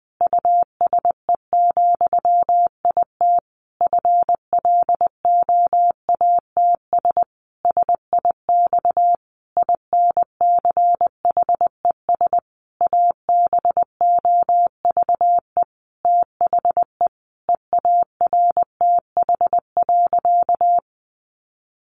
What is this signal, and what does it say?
Morse code, heard as a keyed tone, about 20 wpm, 700 Hz. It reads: USE, IT FLOATS SIX INCHES ABOVE THE EARTH.
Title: A Sound Of Thunder 20wpm